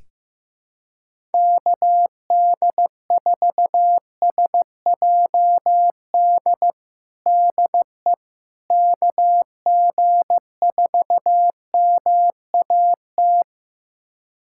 Morse code, heard as a keyed tone, about 15 wpm, 700 Hz. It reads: KD4SJD DE KG4MAT
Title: USA Callsign DE Callsign 15wpm